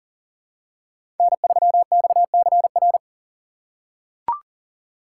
Morse code, heard as a keyed tone, about 40 wpm, 700 Hz. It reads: N3XCR
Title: USA Call Signs - 5 Characters - 4x Character Spacing - 40wpm